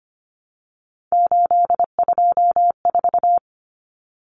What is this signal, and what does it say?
Morse code, heard as a keyed tone, about 25 wpm, 700 Hz. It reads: 824
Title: Common Cut Numbers - 3-Digit 25wpm